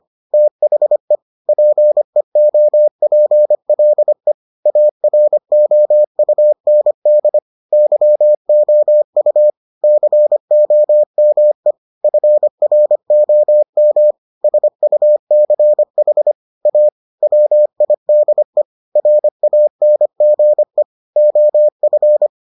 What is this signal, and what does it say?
Morse code, heard as a keyed tone, about 25 wpm, 600 Hz. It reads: THE PEOPLE AROUND YOU COME FROM SUCH A WIDE RANGE OF